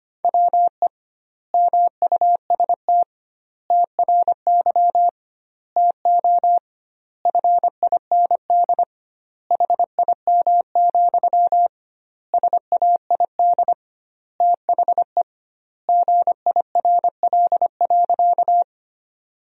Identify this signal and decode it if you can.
Morse code, heard as a keyed tone, about 25 wpm, 700 Hz. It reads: WE MUST TRY TO FIND HIM, SAID THE GIRL.
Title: 2x Word-Spacing - Sentences from Top 400 Words - 25wpm